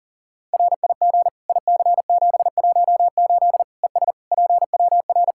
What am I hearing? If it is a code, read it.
Morse code, heard as a keyed tone, about 40 wpm, 700 Hz. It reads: RIG IC718 ES PWR
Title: Practice QSO - Exchange 2 - Info - 40wpm